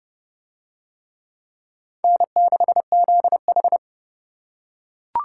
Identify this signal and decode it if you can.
Morse code, heard as a keyed tone, about 29 wpm, 700 Hz. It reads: N6ZH